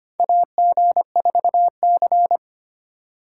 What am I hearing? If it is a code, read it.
Morse code, heard as a keyed tone, about 25 wpm, 700 Hz. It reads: AG4C
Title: USA Call Signs - 4 Characters - 3x Character Spacing - 25wpm